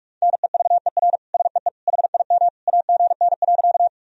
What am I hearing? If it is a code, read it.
Morse code, heard as a keyed tone, about 41 wpm, 700 Hz. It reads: NEVER SEE HIM AGN.
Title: Sentences from Top 300 Words and Abbreviations 45wpm